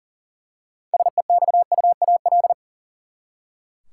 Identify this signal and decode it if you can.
Morse code, heard as a keyed tone, about 39 wpm, 700 Hz. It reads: SEXUAL
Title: Six Letter Words - 40wpm